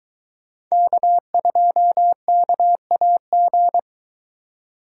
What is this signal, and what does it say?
Morse code, heard as a keyed tone, about 22 wpm, 700 Hz. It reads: K2KAG